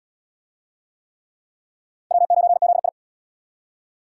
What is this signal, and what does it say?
Morse code, heard as a keyed tone, about 77 wpm, 700 Hz. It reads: W9ZI